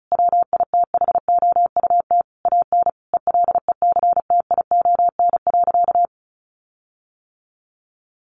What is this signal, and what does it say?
Morse code, heard as a keyed tone, about 34 wpm, 700 Hz. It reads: WITHOUT AN ELECTION.
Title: Sentences from Top 1500 Words 35wpm